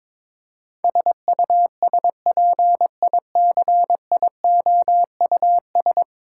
Morse code, heard as a keyed tone, about 22 wpm, 700 Hz. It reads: SUSPICIOUS
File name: practice / 22wpm / 10 Letter Words - First 3 Letters Given - 22wpm